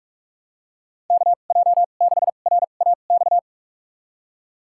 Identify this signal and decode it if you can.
Morse code, heard as a keyed tone, about 43 wpm, spaced as for 32 wpm, 700 Hz. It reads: KJ6RAX
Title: USA Call Signs - 6 Characters - 2x Character Spacing - 45wpm